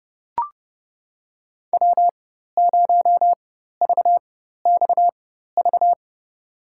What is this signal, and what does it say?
Morse code, heard as a keyed tone, about 32 wpm, spaced as for 14 wpm, 700 Hz. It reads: W0VXV
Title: USA Call Signs - 4x Character Spacing - 30wpm